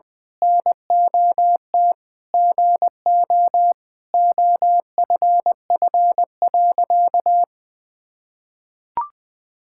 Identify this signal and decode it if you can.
Morse code, heard as a keyed tone, about 21 wpm, 700 Hz. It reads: NOT GO OFF.
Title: Sentences with 4-Word Phrases - 20wpm